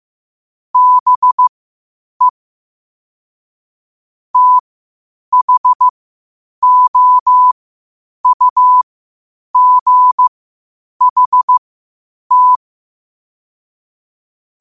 Morse code, heard as a keyed tone, about 15 wpm, spaced as for 8 wpm, 1000 Hz. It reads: BE THOUGHT
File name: qotdfiles / 8wpm